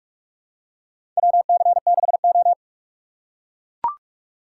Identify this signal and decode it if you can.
Morse code, heard as a keyed tone, about 42 wpm, 700 Hz. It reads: WC6O
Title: USA Call Signs - 4 Characters - 4x Character Spacing - 45wpm